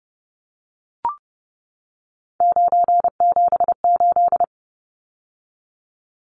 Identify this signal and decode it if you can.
Morse code, heard as a keyed tone, about 30 wpm, 700 Hz. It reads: 978
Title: Numbers - 3-Digit 30wpm